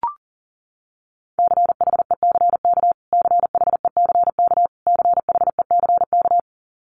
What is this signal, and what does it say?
Morse code, heard as a keyed tone, about 40 wpm, 700 Hz. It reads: CHECK CHECK CHECK
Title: Top 500 Words Repeated 40wpm